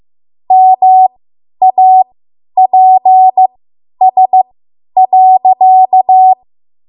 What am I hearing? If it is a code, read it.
Morse code, heard as a keyed tone, about 15 wpm, spaced as for 10 wpm, 750 Hz. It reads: MAPS.
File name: Morse / 10WPM